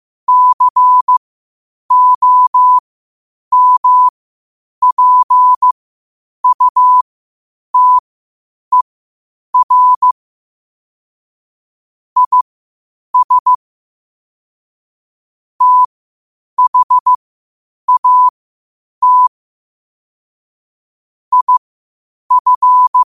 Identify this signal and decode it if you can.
Morse code, heard as a keyed tone, about 15 wpm, spaced as for 8 wpm, 1000 Hz. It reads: COMPUTER IS THAT IF
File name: qotdfiles / 8wpm